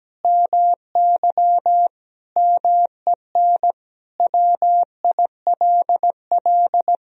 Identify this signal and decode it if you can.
Morse code, heard as a keyed tone, about 17 wpm, 700 Hz. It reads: MY MEN WILL
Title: Sentences from Top 1000 Words - Review Part 1 - 17wpm